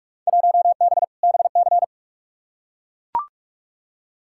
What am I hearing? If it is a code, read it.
Morse code, heard as a keyed tone, about 43 wpm, 700 Hz. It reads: 1B BC
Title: Field Day Exchanges - 45wpm